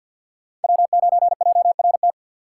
Morse code, heard as a keyed tone, about 52 wpm, 700 Hz. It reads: W9JRT
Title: USA Call Signs - 5 Characters - 6x Character Spacing - 50wpm